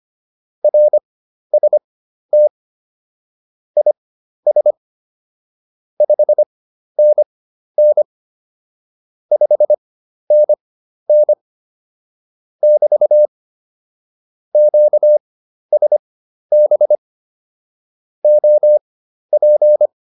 Morse code, heard as a keyed tone, about 25 wpm, spaced as for 12 wpm, 600 Hz. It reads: RST IS 5NN 5NN <BT> QSB OP